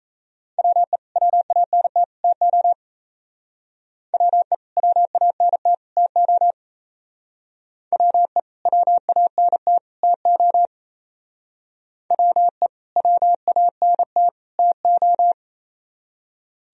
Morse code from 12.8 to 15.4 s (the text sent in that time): WANT TO